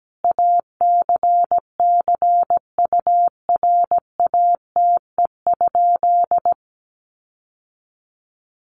Morse code, heard as a keyed tone, about 17 wpm, 700 Hz. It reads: ACCURATE?
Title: Sentences from Top 1200 Words 17wpm